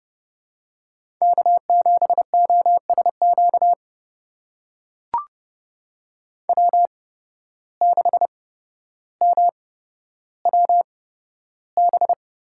Morse code, heard as a keyed tone, about 30 wpm, spaced as for 13 wpm, 700 Hz. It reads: K7OSQ W6MWB
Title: USA Call Signs - 5 Characters - 8x Character Spacing - 30wpm